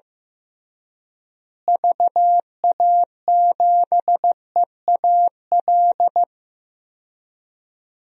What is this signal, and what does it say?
Morse code, heard as a keyed tone, about 15 wpm, 700 Hz. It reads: VA7EAL
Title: DXCC-100 Call Signs - 6 Characters - Repeated 3x - 15wpm